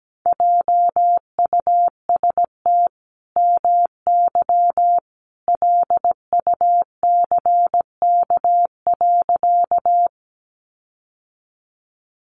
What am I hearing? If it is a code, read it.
Morse code, heard as a keyed tone, about 17 wpm, 700 Hz. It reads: JUST MY LUCK.
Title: Sentences from Top 1500 Words 17wpm